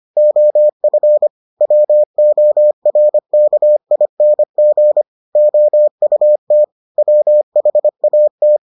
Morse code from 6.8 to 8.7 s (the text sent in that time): WHAT